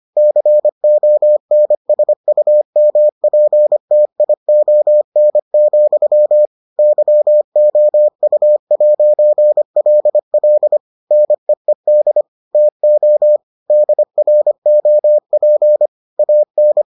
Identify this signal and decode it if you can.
Morse code, heard as a keyed tone, about 25 wpm, 600 Hz. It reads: CONSUMPTION, YOU'LL NEED TO DROP AN